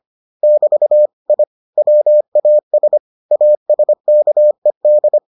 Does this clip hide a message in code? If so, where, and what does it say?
Yes: Morse code from 0.4 to 5.2 s, heard as a keyed tone, about 25 wpm, 600 Hz. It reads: <BT> I WAS ASKED